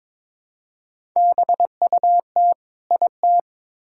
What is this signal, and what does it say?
Morse code, heard as a keyed tone, about 23 wpm, 700 Hz. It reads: BUT IT
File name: practice / 22wpm / Build A Phrase - 2-Word Phrases - 22wpm